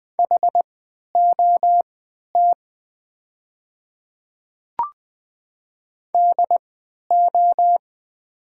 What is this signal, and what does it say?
Morse code, heard as a keyed tone, about 20 wpm, spaced as for 11 wpm, 700 Hz. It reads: HOT DO
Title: 3x Character Spacing - Top 100 Words - 20wpm